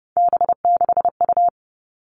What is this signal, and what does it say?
Morse code, heard as a keyed tone, about 30 wpm, 700 Hz. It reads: B6U
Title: Three Challenging Letter-Number Combinations 30wpm